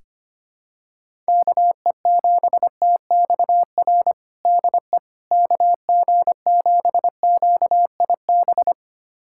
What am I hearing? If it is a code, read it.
Morse code, heard as a keyed tone, about 25 wpm, 700 Hz. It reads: KE7TXR DE KG7QIB
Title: USA Callsign DE Callsign - 6 Characters - 25wpm